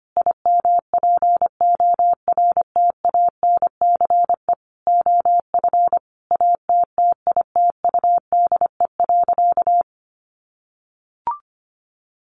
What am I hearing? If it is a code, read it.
Morse code, heard as a keyed tone, about 25 wpm, 700 Hz. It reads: IMPORTANCE OF ATTITUDE.
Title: Sentences from Top 1300 Words 25wpm